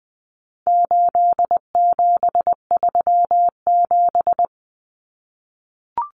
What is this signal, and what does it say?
Morse code, heard as a keyed tone, about 20 wpm, 700 Hz. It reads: 8737
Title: Common Cut Numbers - 4-Digit 20wpm